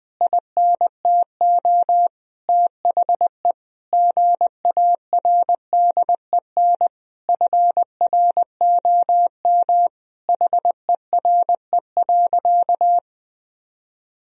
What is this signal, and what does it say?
Morse code, heard as a keyed tone, about 21 wpm, 700 Hz. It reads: INTO THE GARDEN FROM HERE.
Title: Sentences from Top 1000 Words - Review Part 4 - 20wpm